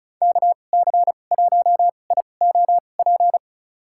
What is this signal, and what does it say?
Morse code, heard as a keyed tone, about 36 wpm, spaced as for 25 wpm, 700 Hz. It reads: KC1IOP
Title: USA Call Signs - 6 Characters - 2x Character Spacing - 35wpm